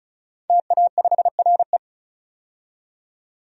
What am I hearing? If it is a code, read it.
Morse code, heard as a keyed tone, about 36 wpm, 700 Hz. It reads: TA5RE
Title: DXCC-100 Call Signs - 5 Characters - 35wpm